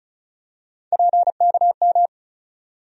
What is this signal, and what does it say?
Morse code, heard as a keyed tone, about 36 wpm, 700 Hz. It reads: PKM
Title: USA Call Sign Postfixes - 3 Letters 35wpm